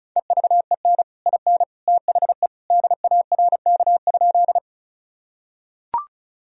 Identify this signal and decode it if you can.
Morse code, heard as a keyed tone, about 34 wpm, 700 Hz. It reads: EVEN IN THE DARK?
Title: Sentences from Top 1000 Words - Review Part 1 - 35wpm